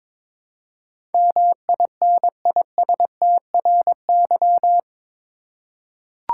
Morse code, heard as a keyed tone, about 21 wpm, 700 Hz. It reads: MINISTRY
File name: practice / 22wpm / Eight Letter Words - 22wpm